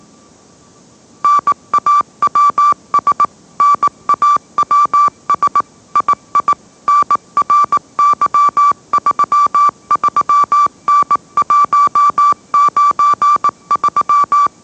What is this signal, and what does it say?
Morse code, heard as a keyed tone, about 18 wpm, 1160 Hz. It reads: NAWS NAWS II NRY33N193